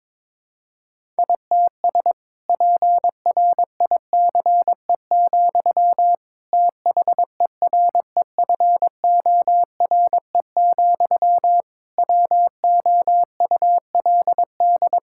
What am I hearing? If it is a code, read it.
Morse code, heard as a keyed tone, about 21 wpm, 700 Hz. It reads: ITS PRICE, THEREFORE, WOULD
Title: Sentences from Top 1000 Words - Review Part 5 - 22wpm